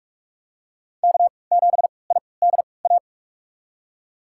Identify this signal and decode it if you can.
Morse code, heard as a keyed tone, about 46 wpm, spaced as for 25 wpm, 700 Hz. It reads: K7IDA
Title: USA Call Signs - 3x Character Spacing - 45wpm